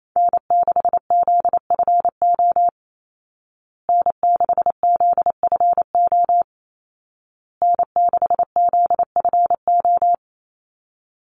Morse code, heard as a keyed tone, about 28 wpm, 700 Hz. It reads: N6ZFO N6ZFO N6ZFO
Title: Call Signs Repeated 2x - 28wpm